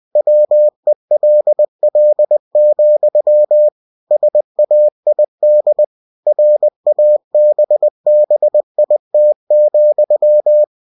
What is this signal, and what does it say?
Morse code, heard as a keyed tone, about 20 wpm, 600 Hz. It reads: WELL, SAID RABBIT,
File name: cw-books / winnie-the-pooh / winnie-20wpm-page-0084